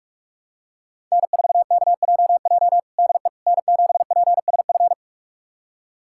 Morse code, heard as a keyed tone, about 46 wpm, 700 Hz. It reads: N4KJJ DE N7PSF
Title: USA Callsign DE Callsign - 5 Characters - 45wpm